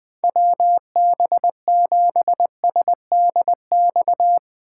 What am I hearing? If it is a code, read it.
Morse code, heard as a keyed tone, about 20 wpm, 700 Hz. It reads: WB7SDX
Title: Speed Racing - USA 2x3 Call Signs 17wpm